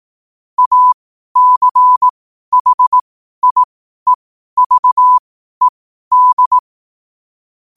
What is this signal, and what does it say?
Morse code, heard as a keyed tone, about 18 wpm, spaced as for 12 wpm, 1000 Hz. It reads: ACHIEVED
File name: qotdfiles / 12wpm